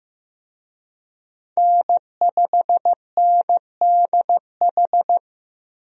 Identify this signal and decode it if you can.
Morse code, heard as a keyed tone, about 15 wpm, 700 Hz. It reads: N5NDH
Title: USA Call Signs - 5 Characters - Repeated 2x - 15wpm